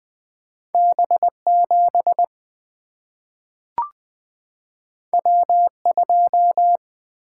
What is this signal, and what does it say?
Morse code, heard as a keyed tone, about 20 wpm, 700 Hz. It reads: B 7 W 2
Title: Single-Letter Single-Number 20wpm